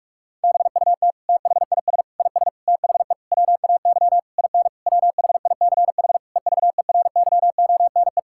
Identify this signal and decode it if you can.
Morse code, heard as a keyed tone, about 43 wpm, 700 Hz. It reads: BUT THIS IS THE WAY IN WHICH EVERYONE
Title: Sentences from Top 1000 Words - Review Part 5 - 45wpm